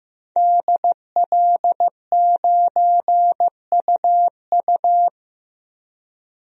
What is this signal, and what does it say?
Morse code, heard as a keyed tone, about 15 wpm, 700 Hz. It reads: DL9UU
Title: DXCC-100 Call Signs - 5 Characters - 15wpm